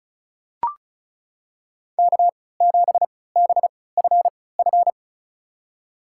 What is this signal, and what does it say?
Morse code, heard as a keyed tone, about 36 wpm, spaced as for 20 wpm, 700 Hz. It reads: K7BFF
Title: USA Call Signs - 5 Characters - 3x Character Spacing - 35wpm